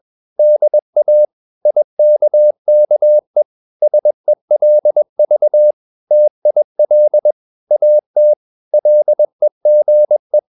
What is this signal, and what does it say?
Morse code, heard as a keyed tone, about 21 wpm, 600 Hz. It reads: DA IKKE SELV TIL AT LEGE
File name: Ferie QRS 21wpm